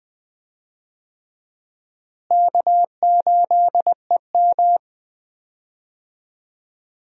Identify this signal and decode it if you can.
Morse code, heard as a keyed tone, about 20 wpm, 700 Hz. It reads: K8EM
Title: USA Call Signs - 4 Characters - Repeated 3x - 20wpm